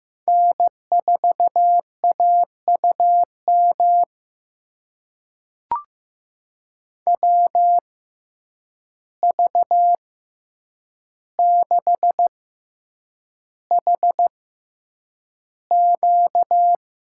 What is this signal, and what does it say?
Morse code, heard as a keyed tone, about 15 wpm, spaced as for 8 wpm, 700 Hz. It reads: N4AUM WV6HQ
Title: USA Call Signs - 5 Characters - 6x Character Spacing - 15wpm